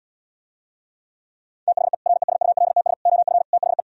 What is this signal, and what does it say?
Morse code, heard as a keyed tone, about 77 wpm, 700 Hz. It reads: THE RESULTS OF THE